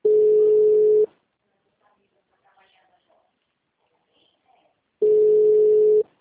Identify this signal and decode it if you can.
Ringback tone (Europe, 425 Hz), with speech.